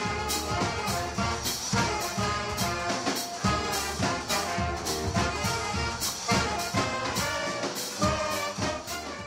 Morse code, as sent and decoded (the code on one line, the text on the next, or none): none